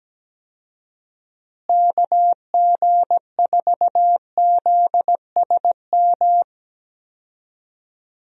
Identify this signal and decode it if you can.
Morse code, heard as a keyed tone, about 17 wpm, 700 Hz. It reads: KG4ZSM
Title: USA Call Signs - 6 Characters - Repeated 2x - 17wpm